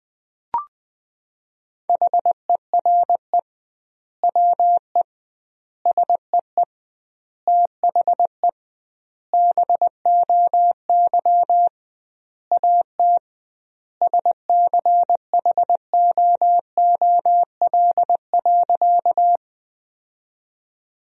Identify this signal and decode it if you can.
Morse code, heard as a keyed tone, about 20 wpm, 700 Hz. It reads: HERE WE SEE THE BOY AT SCHOOL.
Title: 2x Word-Spacing - Sentences from Top 300 Words - 20wpm